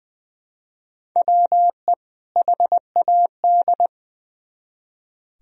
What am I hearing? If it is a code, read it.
Morse code, heard as a keyed tone, about 23 wpm, 700 Hz. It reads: WE HAD
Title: Speed-Racing - Sets of 2 Words 20wpm